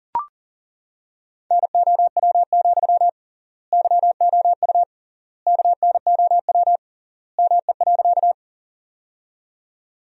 Morse code, heard as a keyed tone, about 40 wpm, 700 Hz. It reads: NOW, YOU KNOW ME.
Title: 3x Word-Spacing - Sentences from Top 200 Words - 40wpm